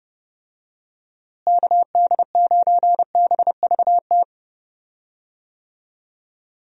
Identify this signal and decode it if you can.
Morse code, heard as a keyed tone, about 32 wpm, 700 Hz. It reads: KD9BVT
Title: USA Call Signs - 6 Characters - Repeated 2x - 30wpm